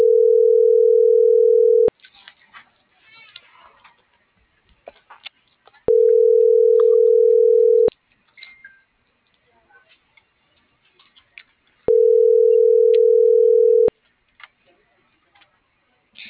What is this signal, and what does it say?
Ringback tone (North America), 440 + 480 Hz.